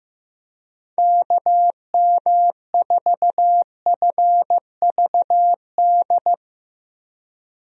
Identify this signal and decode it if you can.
Morse code, heard as a keyed tone, about 15 wpm, 700 Hz. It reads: KM4FVD
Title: USA Call Signs - 6 Characters - 15wpm